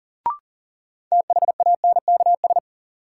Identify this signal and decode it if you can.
Morse code, heard as a keyed tone, about 40 wpm, 700 Hz. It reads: THANKS
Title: Sentences from Top 600 Words - New Word Drill - 40wpm